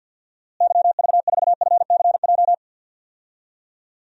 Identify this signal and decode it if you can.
Morse code, heard as a keyed tone, about 50 wpm, 700 Hz. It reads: YV4FCJ
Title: DXCC-100 Call Signs - 6 Characters - Repeated 3x - 50wpm